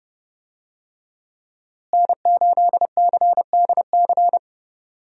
Morse code, heard as a keyed tone, about 30 wpm, 700 Hz. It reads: N8CDC
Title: USA Call Signs - 5 Characters - Repeated 2x - 30wpm